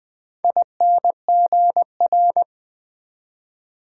Morse code, heard as a keyed tone, about 20 wpm, 700 Hz. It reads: INGR
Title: Build A Word - 4 Letters At A Time - 10-Letter Words - 20wpm